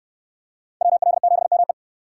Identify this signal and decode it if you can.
Morse code, heard as a keyed tone, about 66 wpm, 700 Hz. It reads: WL7ME